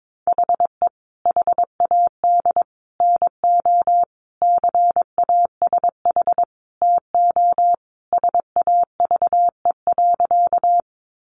Morse code, heard as a keyed tone, about 22 wpm, 700 Hz. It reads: HE HAD NO CASH TO SAVE.
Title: Sentences from Top 1100 Words 22wpm